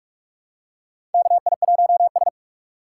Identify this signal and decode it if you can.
Morse code, heard as a keyed tone, about 46 wpm, 700 Hz. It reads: KI1S
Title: USA Call Signs - 4 Characters - 2x Character Spacing - 45wpm